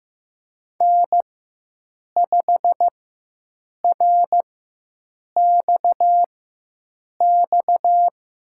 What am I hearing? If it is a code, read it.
Morse code, heard as a keyed tone, about 15 wpm, spaced as for 7 wpm, 700 Hz. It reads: N5RXX
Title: USA Call Signs - 4x Character Spacing - 15wpm